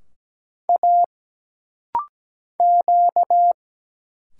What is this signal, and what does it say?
Morse code, heard as a keyed tone, about 17 wpm, 700 Hz. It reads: A Q
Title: Single Letter-Number Alt - Rapid-Fire 17wpm